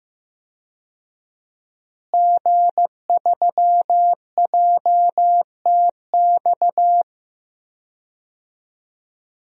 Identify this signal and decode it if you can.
Morse code, heard as a keyed tone, about 15 wpm, 700 Hz. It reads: G3JTX
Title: DXCC-100 Call Signs - 5 Characters - Repeated 3x - 15wpm